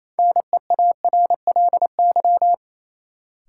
Morse code, heard as a keyed tone, about 28 wpm, 700 Hz. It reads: NEARLY
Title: Sentences from Top 1300 Words - New Word Drill Alt - 28wpm